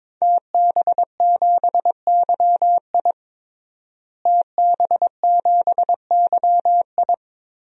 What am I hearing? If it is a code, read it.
Morse code, heard as a keyed tone, about 22 wpm, 700 Hz. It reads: TB7YI TB7YI
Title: DXCC-100 Call Signs - 5 Characters - Repeated 2x - 22wpm